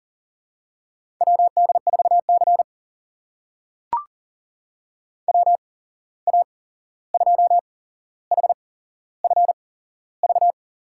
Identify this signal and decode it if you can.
Morse code, heard as a keyed tone, about 40 wpm, spaced as for 15 wpm, 700 Hz. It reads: WD4C WA2HFV
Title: USA Call Signs - 8x Character Spacing - 40wpm